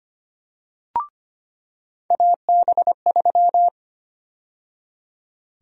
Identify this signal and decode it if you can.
Morse code, heard as a keyed tone, about 25 wpm, 700 Hz. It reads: AB3
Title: USA Call Sign Prefixes - 25wpm